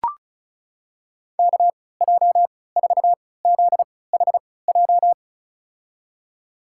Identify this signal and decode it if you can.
Morse code, heard as a keyed tone, about 36 wpm, spaced as for 20 wpm, 700 Hz. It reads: KJ4ZHJ
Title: USA Call Signs - 6 Characters - 3x Character Spacing - 35wpm